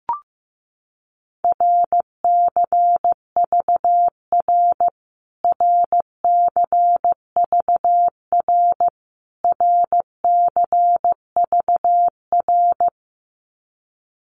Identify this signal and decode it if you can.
Morse code, heard as a keyed tone, about 15 wpm, 700 Hz. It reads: RCVR RCVR RCVR
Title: Abbreviations Repeated 3x - 15wpm